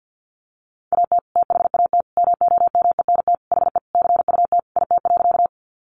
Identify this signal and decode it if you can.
Morse code, heard as a keyed tone, about 48 wpm, 700 Hz. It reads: AT THAT MOMENT HE CUT IT.
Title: Sentences from Top 1000 Words 50wpm